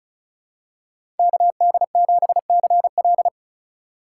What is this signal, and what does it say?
Morse code, heard as a keyed tone, about 36 wpm, 700 Hz. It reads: KD7CL
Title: USA Call Signs - 5 Characters - 2x Character Spacing - 35wpm